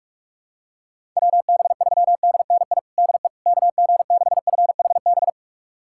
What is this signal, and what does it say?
Morse code, heard as a keyed tone, about 42 wpm, 700 Hz. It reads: WB3DNI DE KG6FHB
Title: USA Callsign DE Callsign - 6 Characters - 45wpm